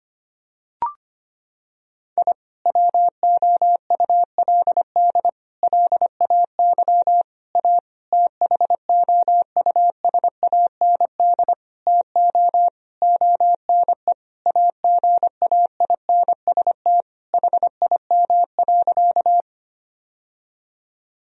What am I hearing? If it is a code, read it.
Morse code, heard as a keyed tone, about 25 wpm, 700 Hz. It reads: I WOULD LAY A THOUSAND TO ONE AGAINST HIM.
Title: Sentences from Top 1000 Words - Review Part 3 - 25wpm